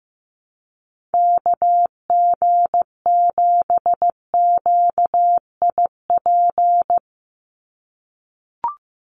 Morse code, heard as a keyed tone, about 15 wpm, 700 Hz. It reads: KG7QIP
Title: USA Call Signs 15wpm